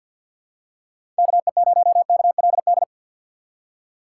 Morse code, heard as a keyed tone, about 52 wpm, 690 Hz. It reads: KE0KLD
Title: USA Call Signs - 6 Characters - 3x Character Spacing - 50wpm